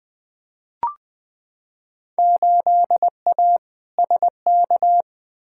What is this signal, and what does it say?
Morse code, heard as a keyed tone, about 20 wpm, 700 Hz. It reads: 8A SK
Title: Field Day Exchanges Encore - 20wpm